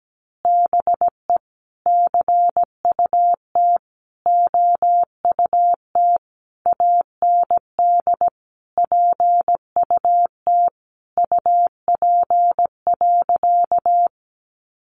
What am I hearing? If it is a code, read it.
Morse code, heard as a keyed tone, about 17 wpm, 700 Hz. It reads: BE CUT OUT AND PUT UP.
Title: A Sound Of Thunder 17wpm